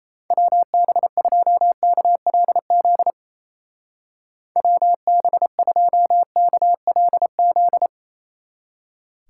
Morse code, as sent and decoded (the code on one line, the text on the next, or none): .-- -... ..--- -.- .-.. --.. / .-- -... ..--- -.- .-.. --..
WB2KLZ WB2KLZ